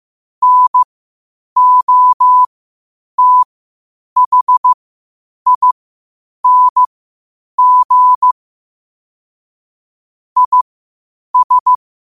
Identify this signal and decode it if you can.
Morse code, heard as a keyed tone, about 15 wpm, spaced as for 8 wpm, 1000 Hz. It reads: NOTHING IS
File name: qotdfiles / 8wpm